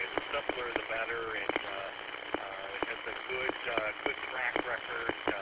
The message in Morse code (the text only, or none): none